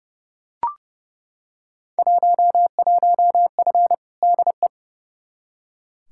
Morse code, heard as a keyed tone, about 30 wpm, 700 Hz. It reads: 11F DE
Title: Field Day Exchanges Encore - 30wpm